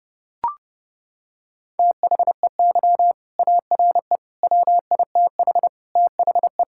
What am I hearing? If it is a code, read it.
Morse code, heard as a keyed tone, about 30 wpm, 700 Hz. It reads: THEY ARE WITH THE
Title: Sentences from Top 1000 Words - Review Part 5 - 30wpm